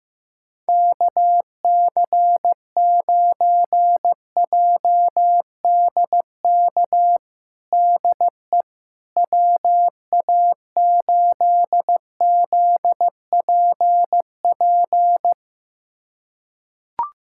Morse code, heard as a keyed tone, about 15 wpm, 700 Hz. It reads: KC9JDK DE WA8ZPP
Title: USA Callsign DE Callsign - 6 Characters - 15wpm